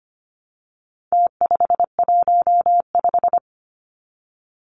Morse code, heard as a keyed tone, about 25 wpm, 700 Hz. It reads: T515
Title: Numbers - Contest Style - 4-Digit 25wpm